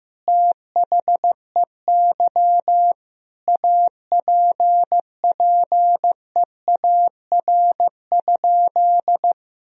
Morse code, heard as a keyed tone, about 16 wpm, 700 Hz. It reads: THEY APPEAR?
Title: Sentences from Top 1000 Words - Review Part 2 - 15wpm